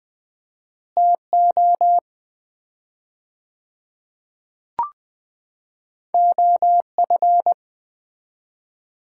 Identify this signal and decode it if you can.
Morse code, heard as a keyed tone, about 20 wpm, 700 Hz. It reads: TO OF
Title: 4x Word-Spacing - Sets of 2 Words - 20wpm